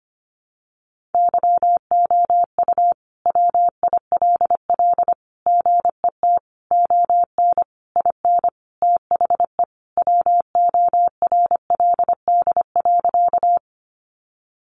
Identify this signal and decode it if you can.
Morse code, heard as a keyed tone, about 25 wpm, 700 Hz. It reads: YOU WILL GET ON IN THE WORLD.
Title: Sentences from Top 200 Words 25wpm